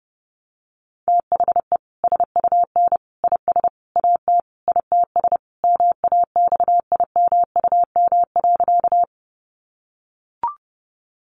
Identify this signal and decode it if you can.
Morse code, heard as a keyed tone, about 30 wpm, 700 Hz. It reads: THE SUN IS AT ITS MAXIMUM.
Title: Sentences from Top 1300 Words 30wpm